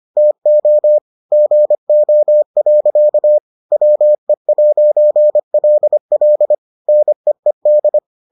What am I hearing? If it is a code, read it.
Morse code, heard as a keyed tone, about 25 wpm, 600 Hz. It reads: TO GO. WE'LL NEED